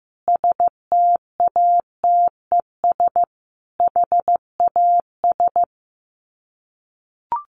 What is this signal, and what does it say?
Morse code, heard as a keyed tone, about 15 wpm, 700 Hz. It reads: STATES HAS
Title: Sets of 5 Words Encore 15wpm